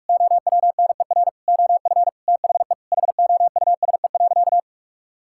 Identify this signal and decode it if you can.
Morse code, heard as a keyed tone, about 46 wpm, 700 Hz. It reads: OWNER OF THE HOUSE.
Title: Sentences from Top 1600 Words 45wpm